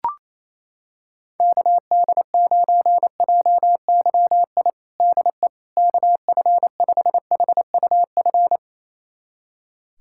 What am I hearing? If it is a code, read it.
Morse code, heard as a keyed tone, about 28 wpm, 700 Hz. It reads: KD9JYI DE KF5HUF
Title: USA Callsign DE Callsign 28wpm